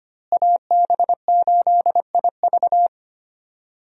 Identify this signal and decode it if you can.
Morse code, heard as a keyed tone, about 25 wpm, 700 Hz. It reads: AB8IV
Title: USA Call Signs - 5 Characters - Repeated 3x - 25wpm